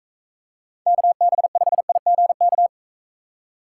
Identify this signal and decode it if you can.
Morse code, heard as a keyed tone, about 42 wpm, 700 Hz. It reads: KB5IGK